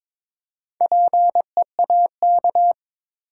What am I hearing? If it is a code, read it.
Morse code, heard as a keyed tone, about 21 wpm, 700 Hz. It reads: PEAK
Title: Four Letter Words - 22wpm